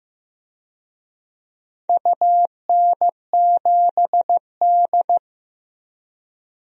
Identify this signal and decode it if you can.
Morse code, heard as a keyed tone, about 15 wpm, 700 Hz. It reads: UN7D
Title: DXCC-100 Call Signs - 4 Characters - 15wpm